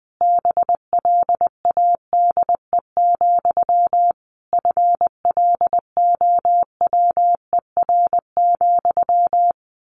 Morse code, heard as a keyed tone, about 20 wpm, 700 Hz. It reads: BLADE, FLOWER,
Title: A Sound Of Thunder 20wpm